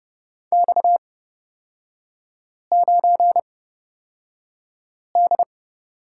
Morse code, heard as a keyed tone, about 30 wpm, spaced as for 5 wpm, 700 Hz. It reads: X9D
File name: Lesson 37e - Letter x - Three Characters 30wpm